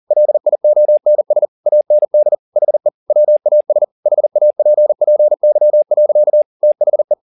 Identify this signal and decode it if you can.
Morse code, heard as a keyed tone, about 40 wpm, 600 Hz. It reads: LIONS AND HE WAS HAPPY. THE
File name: the old man and the sea 2 40wpm